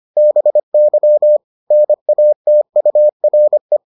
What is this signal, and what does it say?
Morse code, heard as a keyed tone, about 25 wpm, 600 Hz. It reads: BY NATURE